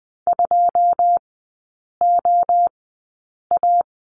Morse code, heard as a keyed tone, about 20 wpm, spaced as for 8 wpm, 690 Hz. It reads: 2OA
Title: 081 Lesson 14e - Number 2 - Three Characters 20wpm